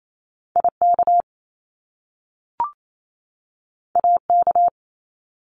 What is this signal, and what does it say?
Morse code, heard as a keyed tone, about 28 wpm, 700 Hz. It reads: IK AK
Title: Top Two Letter Combinations in Words 28wpm